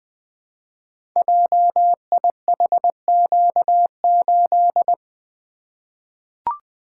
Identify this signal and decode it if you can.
Morse code, heard as a keyed tone, about 20 wpm, 700 Hz. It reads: JIHQ8
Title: Code Groups - 5 Letter:Numbers - 20wpm